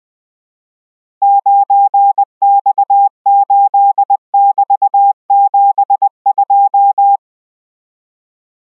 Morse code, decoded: 9X8<BT>72